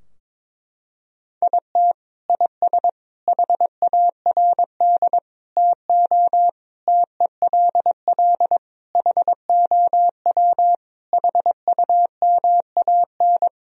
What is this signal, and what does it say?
Morse code, heard as a keyed tone, about 21 wpm, 700 Hz. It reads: IT IS HARD TO TELL HOW HUMAN
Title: Sentences from Top 1000 Words - Review Part 3 - 22wpm